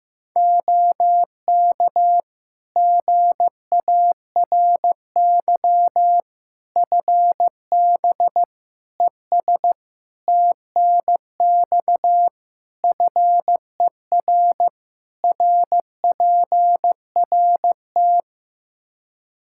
Morse code, heard as a keyed tone, about 15 wpm, 700 Hz. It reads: OK GARY FB ES TNX FER RPRT
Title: Practice QSO - Exchange 2 - Info - 15wpm